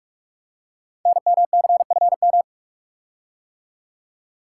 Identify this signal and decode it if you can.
Morse code, heard as a keyed tone, about 45 wpm, 700 Hz. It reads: NMCFM
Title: Code Groups - 5 Letters - 45wpm